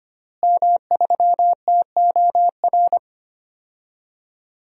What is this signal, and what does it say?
Morse code, heard as a keyed tone, about 25 wpm, 700 Hz. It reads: M3TOR
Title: DXCC-100 Call Signs - 5 Characters - Repeated 3x - 25wpm